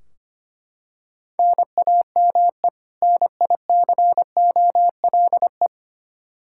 Morse code, heard as a keyed tone, about 25 wpm, 700 Hz. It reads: NAME NICOLE
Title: Practice QSO - Exchange 1 - Beginning - 25wpm